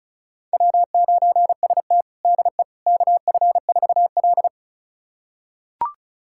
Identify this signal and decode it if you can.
Morse code, heard as a keyed tone, about 34 wpm, 700 Hz. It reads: W9ST DE KF4L
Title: USA Callsign DE Callsign - 4 Characters - 35wpm